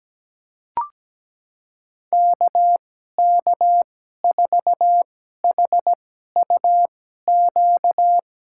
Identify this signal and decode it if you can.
Morse code, heard as a keyed tone, about 17 wpm, spaced as for 12 wpm, 700 Hz. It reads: KK4HUQ
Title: USA Call Signs - 6 Characters - 2x Character Spacing - 17wpm